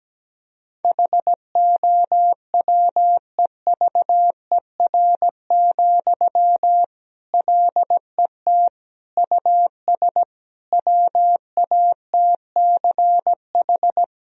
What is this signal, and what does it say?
Morse code, heard as a keyed tone, about 17 wpm, 700 Hz. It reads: HOWEVER, LET US WATCH
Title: Sentences from Top 1000 Words - Review Part 1 - 17wpm